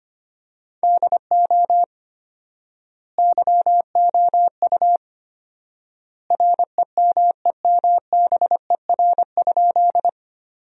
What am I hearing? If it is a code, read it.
Morse code, heard as a keyed tone, about 25 wpm, 700 Hz. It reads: DO YOU REMEMBER?
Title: 4x Word-Spacing - Sentences from Top 400 Words - 25wpm